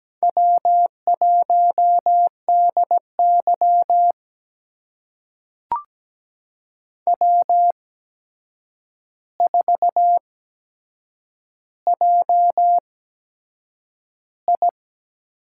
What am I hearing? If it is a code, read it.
Morse code, heard as a keyed tone, about 17 wpm, spaced as for 7 wpm, 700 Hz. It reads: W1DY W4JI
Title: USA Call Signs - 4 Characters - 8x Character Spacing - 17wpm